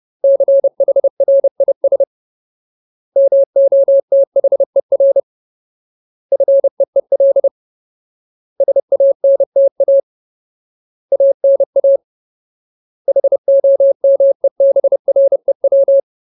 Morse code, heard as a keyed tone, about 30 wpm, 550 Hz. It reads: CHRIS MOTHER FEEL SANTA ANA HOMEBREW
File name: Ultimate Word List / 30WPM Ultimate